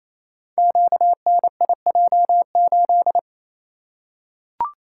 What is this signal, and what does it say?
Morse code, heard as a keyed tone, about 27 wpm, 700 Hz. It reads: QNIJ8
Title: Code Groups - 5 Letter:Numbers - 28wpm